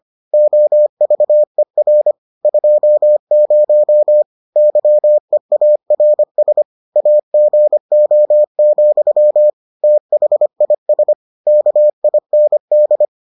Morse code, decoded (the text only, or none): OVER 20 YEARS AGO, THIS KIND